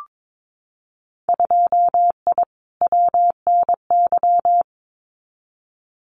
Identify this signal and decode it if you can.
Morse code, heard as a keyed tone, about 22 wpm, 700 Hz. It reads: 2I WNY
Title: Winter Field Day Exchanges Encore 22wpm